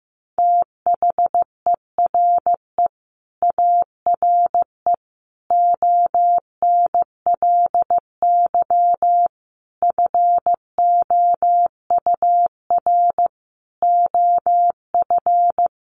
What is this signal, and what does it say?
Morse code, heard as a keyed tone, about 15 wpm, 700 Hz. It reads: THERE ARE ONLY FOUR OF
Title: Sentences from Top 300 Words 15wpm